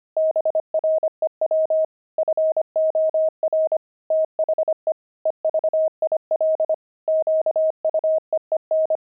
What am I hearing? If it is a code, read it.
Morse code, heard as a keyed tone, about 25 wpm, 630 Hz. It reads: BREW FOR THE EVIL QUEEN
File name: pangram-fs-25-cs-25-grumpy